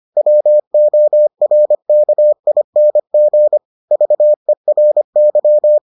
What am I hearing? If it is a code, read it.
Morse code, heard as a keyed tone, about 25 wpm, 600 Hz. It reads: WORKING VERY